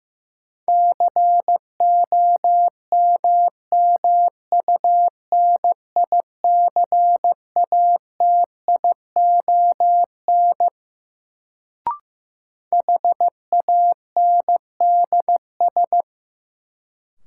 Morse code, heard as a keyed tone, about 15 wpm, 700 Hz. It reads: COMMUNICATION HANDS
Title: Sentences from Top 1000 Words - Review Part 6 - New Word Drill - 15wpm